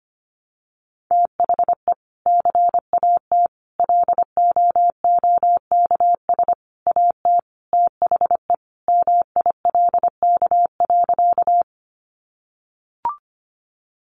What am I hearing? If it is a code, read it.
Morse code, heard as a keyed tone, about 25 wpm, 700 Hz. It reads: THE CAT LOOKS AT THE MILK.
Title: Additional Easy Phrases 25wpm